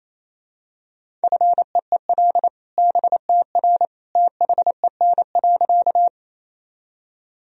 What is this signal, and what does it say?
Morse code, heard as a keyed tone, about 27 wpm, 700 Hz. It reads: FEEL BTR THEN.
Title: Sentences from Top 400 Words and Abbreviations 28wpm